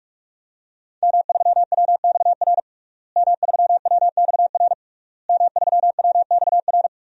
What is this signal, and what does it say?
Morse code, heard as a keyed tone, about 41 wpm, 700 Hz. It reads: M3WXR M3WXR M3WXR
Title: DXCC-100 Call Signs - 5 Characters - Repeated 3x - 45wpm